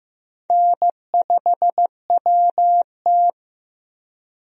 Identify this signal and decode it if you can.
Morse code, heard as a keyed tone, about 15 wpm, 700 Hz. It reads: N5WT
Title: USA Call Signs - 4 Characters - 15wpm